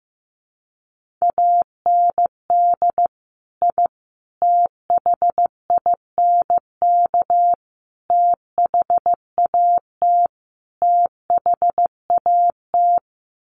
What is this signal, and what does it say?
Morse code, heard as a keyed tone, about 15 wpm, 700 Hz. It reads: AND I THINK THAT THAT
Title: Sets of 5 Words Encore 15wpm